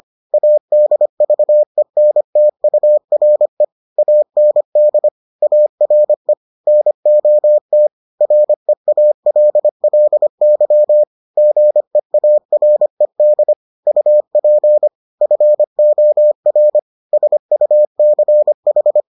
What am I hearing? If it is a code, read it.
Morse code, heard as a keyed tone, about 25 wpm, 600 Hz. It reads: ADVENTURE AND ARE NOT REALLY GEARED UP FOR SUCH